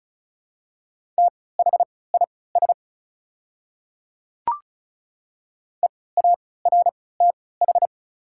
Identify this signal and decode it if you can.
Morse code, heard as a keyed tone, about 36 wpm, spaced as for 20 wpm, 700 Hz. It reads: THIS EARTH
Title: 3x Character Spacing - Top 200 Words - 35wpm